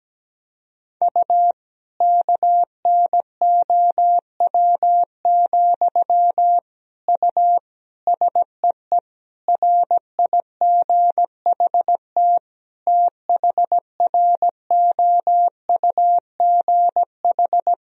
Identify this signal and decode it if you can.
Morse code, heard as a keyed tone, about 17 wpm, 700 Hz. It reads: U KNOW, U SEE RIGHT THROUGH
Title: Sentences from Top 200 Words and Abbreviations 17wpm